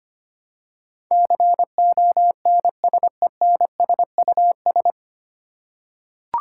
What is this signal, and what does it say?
Morse code, heard as a keyed tone, about 25 wpm, 700 Hz. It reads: CONSENSUS
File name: practice / 25wpm / 9 Letter Words - First 3 Letters Given - 25wpm